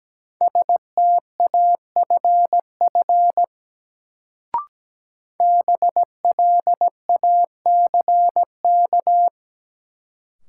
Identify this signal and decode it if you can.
Morse code, heard as a keyed tone, about 17 wpm, 700 Hz. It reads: STAFF BLACK
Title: Sets of 5 Words Encore - New Word Drill - 17wpm